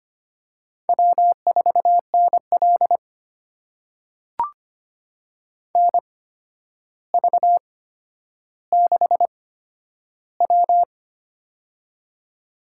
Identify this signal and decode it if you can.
Morse code, heard as a keyed tone, about 25 wpm, spaced as for 11 wpm, 700 Hz. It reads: W4NL NV6W
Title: USA Call Signs - 8x Character Spacing - 25wpm